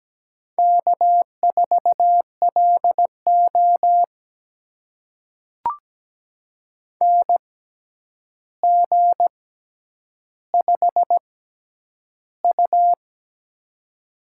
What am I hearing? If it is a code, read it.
Morse code, heard as a keyed tone, about 17 wpm, spaced as for 9 wpm, 700 Hz. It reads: K4LO NG5U
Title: USA Call Signs - 4 Characters - 6x Character Spacing - 17wpm